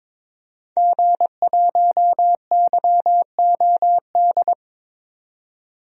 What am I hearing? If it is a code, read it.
Morse code, heard as a keyed tone, about 22 wpm, 700 Hz. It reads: G1YOD
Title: DXCC-100 Call Signs - 5 Characters - 22wpm